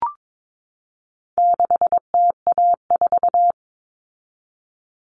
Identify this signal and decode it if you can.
Morse code, heard as a keyed tone, about 22 wpm, 700 Hz. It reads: 6TA4
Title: All Cut Numbers - 4-Digit 22wpm